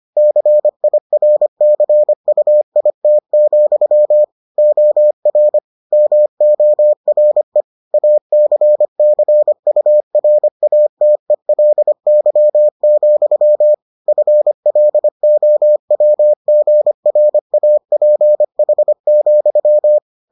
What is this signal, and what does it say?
Morse code, heard as a keyed tone, about 25 wpm, 600 Hz. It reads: CIRCUIT, OR MORE ACCURATELY, FLOWGRAPH,